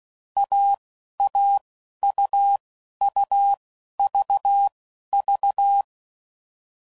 Morse code, heard as a keyed tone, about 16 wpm, spaced as for 11 wpm, 800 Hz. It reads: AAUUVV
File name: Combinations / Difficult Combos / Difficult Combos